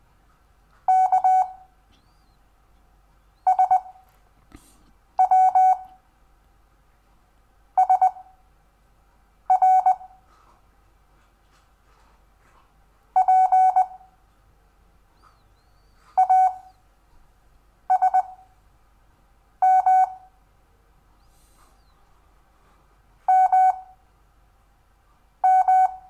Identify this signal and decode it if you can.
Morse code, heard as a keyed tone, about 20 wpm, spaced as for 5 wpm, 750 Hz. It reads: K SW SR P ASM MM
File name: cw / Lesson6 WI